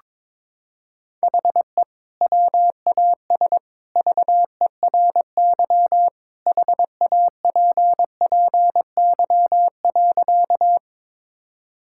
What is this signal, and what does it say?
Morse code, heard as a keyed tone, about 22 wpm, 700 Hz. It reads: HE WAS VERY HAPPY.
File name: practice / 22wpm / Sentences with 2-Word Phrases - 22wpm